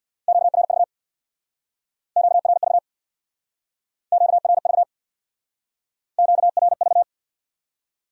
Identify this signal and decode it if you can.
Morse code, heard as a keyed tone, about 65 wpm, 700 Hz. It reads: QRV QRV QRV QRV